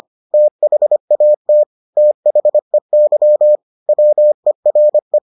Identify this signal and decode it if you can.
Morse code, heard as a keyed tone, about 25 wpm, 600 Hz. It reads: THAT THEY WERE